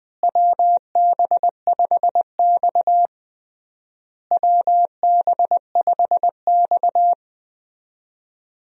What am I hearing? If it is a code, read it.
Morse code, heard as a keyed tone, about 21 wpm, 700 Hz. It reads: WB5X WB5X
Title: USA Call Signs - 4 Characters - Repeated 3x - 20wpm